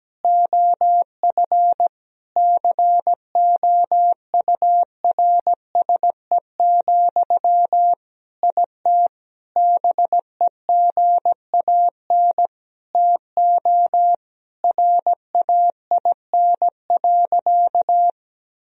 Morse code, heard as a keyed tone, about 17 wpm, 700 Hz. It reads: OF COURSE, IT BEGAN TO RAIN.
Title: Sentences from Top 1000 Words - Review Part 1 - 17wpm